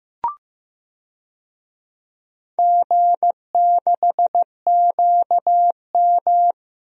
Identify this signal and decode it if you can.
Morse code, heard as a keyed tone, about 15 wpm, 700 Hz. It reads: G6QM
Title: DXCC-100 Call Signs - 4 Characters - Repeated 2x - 15wpm